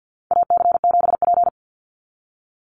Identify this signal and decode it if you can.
Morse code, heard as a keyed tone, about 50 wpm, 700 Hz. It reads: AC7P
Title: NA SOTA Chaser Callsigns 50wpm